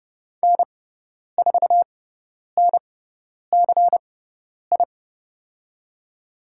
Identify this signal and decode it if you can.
Morse code, heard as a keyed tone, about 30 wpm, spaced as for 10 wpm, 700 Hz. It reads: N4NCI
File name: Lesson 18g - Letter m - Call Signs 30wpm